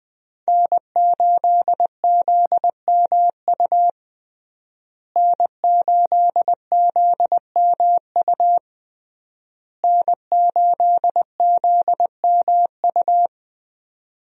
Morse code, heard as a keyed tone, about 20 wpm, 700 Hz. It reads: N8ZMU N8ZMU N8ZMU
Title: USA Call Signs - 5 Characters - Repeated 3x - 20wpm